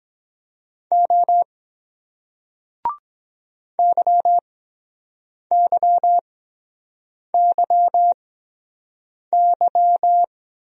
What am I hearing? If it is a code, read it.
Morse code, heard as a keyed tone, about 23 wpm, spaced as for 7 wpm, 700 Hz. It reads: O YYYY